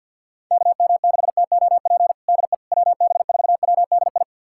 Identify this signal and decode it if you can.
Morse code, heard as a keyed tone, about 52 wpm, 700 Hz. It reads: KM6TOP DE WB4WDI
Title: USA Callsign DE Callsign - 6 Characters - 50wpm